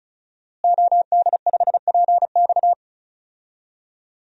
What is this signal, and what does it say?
Morse code, heard as a keyed tone, about 36 wpm, 700 Hz. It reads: OD5PX
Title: DXCC-100 Call Signs - 5 Characters - Repeated 3x - 35wpm